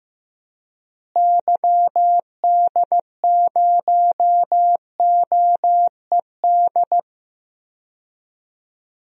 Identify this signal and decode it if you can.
Morse code, heard as a keyed tone, about 15 wpm, 700 Hz. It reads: YD0OED
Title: DXCC-100 Call Signs - 6 Characters - Repeated 3x - 15wpm